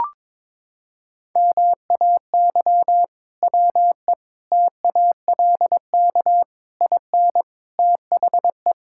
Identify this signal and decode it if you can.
Morse code, heard as a keyed tone, about 22 wpm, 700 Hz. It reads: MAY WE TALK IN THE
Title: Sentences from Top 1600 Words 22wpm